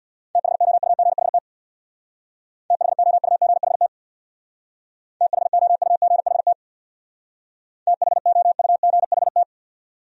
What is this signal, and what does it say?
Morse code, heard as a keyed tone, about 65 wpm, 700 Hz. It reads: THOUGHT THOUGHT THOUGHT THOUGHT